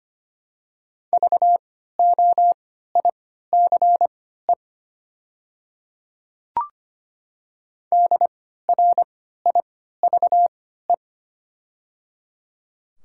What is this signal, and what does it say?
Morse code, heard as a keyed tone, about 25 wpm, spaced as for 14 wpm, 700 Hz. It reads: VOICE DRIVE
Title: 3x Character Spacing - Top 500 Words - 25wpm